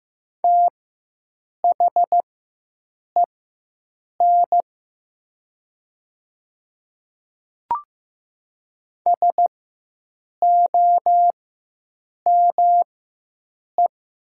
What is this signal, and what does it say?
Morse code, heard as a keyed tone, about 15 wpm, spaced as for 7 wpm, 700 Hz. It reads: THEN SOME
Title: 4x Character Spacing - Top 100 Words - 15wpm